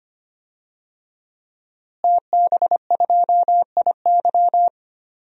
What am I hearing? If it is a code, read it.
Morse code, heard as a keyed tone, about 25 wpm, 700 Hz. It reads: TB2IY
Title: DXCC-100 Call Signs - 5 Characters - Repeated 2x - 25wpm